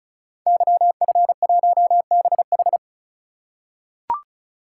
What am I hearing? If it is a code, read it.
Morse code, heard as a keyed tone, about 36 wpm, 690 Hz. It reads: YF1BH
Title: Code Groups - 5 Letter:Numbers - 35wpm